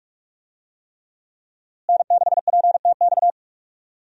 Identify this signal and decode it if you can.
Morse code, heard as a keyed tone, about 46 wpm, 700 Hz. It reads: N6PTX
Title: USA Call Signs - 5 Characters - Repeated 3x - 45wpm